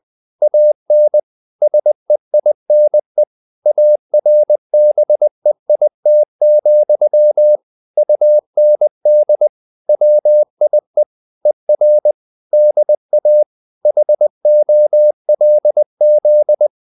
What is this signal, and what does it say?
Morse code, heard as a keyed tone, about 20 wpm, 600 Hz. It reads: AN SEINE ARBEIT, UND WIE ER DA HOLZ